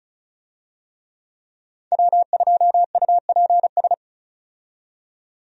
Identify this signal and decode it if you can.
Morse code, heard as a keyed tone, about 36 wpm, 700 Hz. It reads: W2UPS
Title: USA Call Signs - 5 Characters - 35wpm